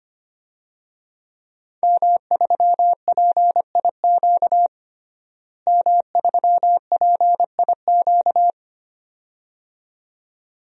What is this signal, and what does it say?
Morse code, heard as a keyed tone, about 25 wpm, 700 Hz. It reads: M3PIQ M3PIQ
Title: DXCC-100 Call Signs - 5 Characters - Repeated 2x - 25wpm